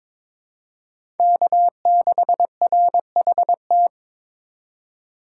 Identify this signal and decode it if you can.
Morse code, heard as a keyed tone, about 22 wpm, 700 Hz. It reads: K6RHT
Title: USA Call Signs - 4x Character Spacing - 22wpm